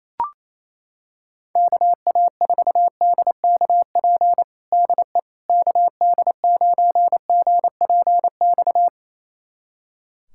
Morse code, heard as a keyed tone, about 28 wpm, 700 Hz. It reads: KA4DKP DE KD9GPX
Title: USA Callsign DE Callsign 28wpm